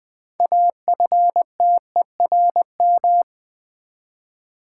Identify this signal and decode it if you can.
Morse code, heard as a keyed tone, about 20 wpm, 700 Hz. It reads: AFTERM
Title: Build A Word - 3 Letters At A Time - 9-Letter Words - 20wpm